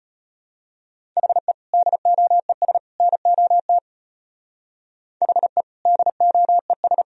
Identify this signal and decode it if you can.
Morse code, heard as a keyed tone, about 31 wpm, 700 Hz. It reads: HE DOES NOT HE DOES
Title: Speed-Racing - Sets of 3 Words 25wpm